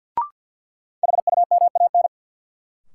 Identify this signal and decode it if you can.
Morse code, heard as a keyed tone, about 48 wpm, 700 Hz. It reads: HUMAN
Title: Sets of 5 Words - New Word Drill - 50wpm